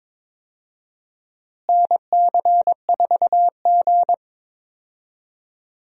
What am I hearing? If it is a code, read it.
Morse code, heard as a keyed tone, about 22 wpm, 700 Hz. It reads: NC4G
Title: USA Call Signs - 4 Characters - Repeated 2x - 22wpm